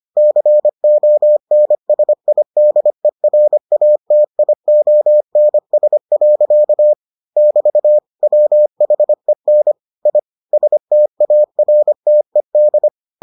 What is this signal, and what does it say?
Morse code, heard as a keyed tone, about 25 wpm, 600 Hz. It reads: CONSIDERATIONS. <BT> WHEN I STARTED